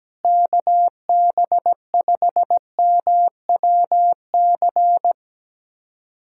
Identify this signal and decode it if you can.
Morse code, heard as a keyed tone, about 17 wpm, 700 Hz. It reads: KB5MWC
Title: USA Callsign DE Callsign - 6 Characters - 17wpm